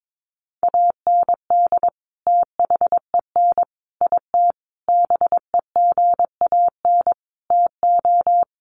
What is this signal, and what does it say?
Morse code, heard as a keyed tone, about 22 wpm, 700 Hz. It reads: AND THEN IT BEGAN TO
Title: Sentences from Top 1200 Words 22wpm